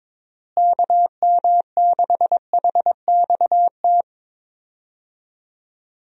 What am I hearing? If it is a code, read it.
Morse code, heard as a keyed tone, about 22 wpm, 700 Hz. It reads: KM6HXT
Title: USA Call Signs - 6 Characters - Repeated 2x - 22wpm